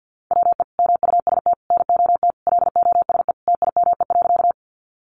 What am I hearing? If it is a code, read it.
Morse code, heard as a keyed tone, about 50 wpm, 700 Hz. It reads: WE MUST NOT LOSE TIME.
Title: Sentences from Top 1100 Words 50wpm